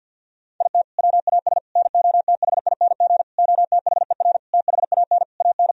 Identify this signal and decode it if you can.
Morse code, heard as a keyed tone, about 46 wpm, 700 Hz. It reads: IT WAS NOTHING OTHER THAN AN
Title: Sentences from Top 1000 Words - Review Part 2 - 50wpm